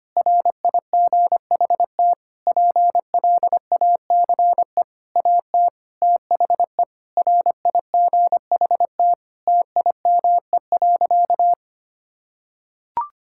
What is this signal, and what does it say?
Morse code, heard as a keyed tone, about 25 wpm, 700 Hz. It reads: RIGHT PLACE AT THE RIGHT TIME.
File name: practice / 25wpm / Sentences with 9-Word Phrases - 25wpm